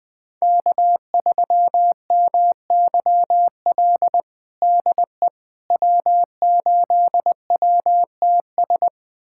Morse code, heard as a keyed tone, about 21 wpm, 700 Hz. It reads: K3MYL DE W8WTS
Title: USA Callsign DE Callsign - 5 Characters - 20wpm